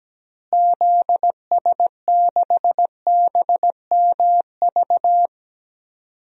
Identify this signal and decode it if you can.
Morse code, heard as a keyed tone, about 17 wpm, 700 Hz. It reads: ZS6BMV
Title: DXCC-100 Call Signs - 6 Characters - Repeated 3x - 17wpm